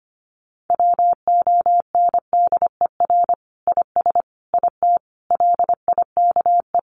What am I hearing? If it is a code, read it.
Morse code, heard as a keyed tone, about 25 wpm, 700 Hz. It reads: WONDER IS IT LIKE
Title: Sentences from Top 500 Words 25wpm